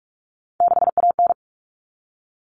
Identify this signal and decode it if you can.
Morse code, heard as a keyed tone, about 44 wpm, 700 Hz. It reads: 6AN
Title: All Cut Numbers - 3-Digit 45wpm